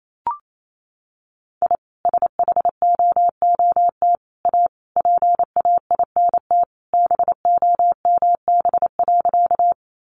Morse code, heard as a keyed tone, about 28 wpm, 700 Hz. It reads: I SHOOT A PAINT BOMB.
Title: A Sound Of Thunder 28wpm